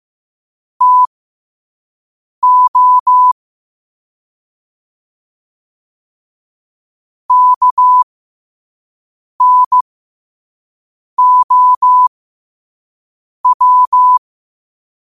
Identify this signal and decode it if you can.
Morse code, heard as a keyed tone, about 15 wpm, spaced as for 5 wpm, 1000 Hz. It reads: TO KNOW
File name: qotdfiles / 5wpm